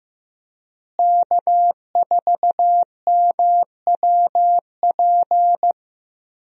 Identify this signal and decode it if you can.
Morse code, heard as a keyed tone, about 15 wpm, 700 Hz. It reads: K4MWP
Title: USA Call Signs - 5 Characters - Repeated 3x - 15wpm